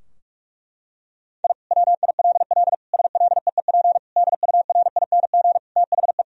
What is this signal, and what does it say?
Morse code, heard as a keyed tone, about 46 wpm, 700 Hz. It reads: I WILL SLEEP DURING THE
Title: Sentences from Top 1000 Words - Review Part 2 - 45wpm